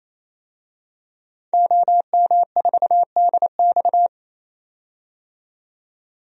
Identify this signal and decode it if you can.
Morse code, heard as a keyed tone, about 28 wpm, 700 Hz. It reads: OM4DX
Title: DXCC-100 Call Signs - 5 Characters - 28wpm